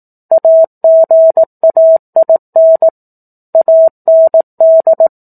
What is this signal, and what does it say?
Morse code, heard as a keyed tone, about 18 wpm, 650 Hz. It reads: AGAIN AND
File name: 04-Chapter-04 18 WPM